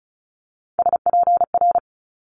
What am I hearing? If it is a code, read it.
Morse code, heard as a keyed tone, about 36 wpm, 700 Hz. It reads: SPR
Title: Multi-letter Phonemes Quick Fire 35wpm